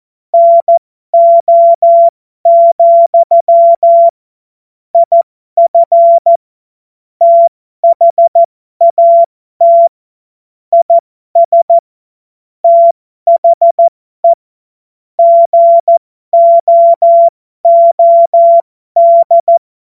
Morse code, heard as a keyed tone, about 14 wpm, spaced as for 12 wpm, 680 Hz. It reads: NO, IF THAT IS THE GOOD
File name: Morsekurs / Witze